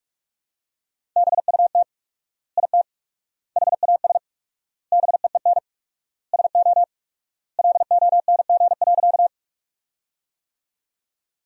Morse code, heard as a keyed tone, about 46 wpm, 700 Hz. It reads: BUT IT HAS BEEN SO LONG.
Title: 4x Word-Spacing - Sentences from Top 100 Words - 45wpm